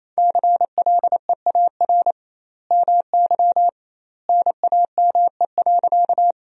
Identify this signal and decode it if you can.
Morse code, heard as a keyed tone, about 28 wpm, 700 Hz. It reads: CLEAR MY NAME.
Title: 2x Word-Spacing - Sentences from Top 500 Words - 28wpm